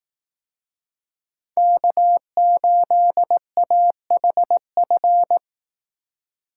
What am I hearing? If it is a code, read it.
Morse code, heard as a keyed tone, about 18 wpm, 700 Hz. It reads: K8AHF
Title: Speed Racing - USA Call Signs 15wpm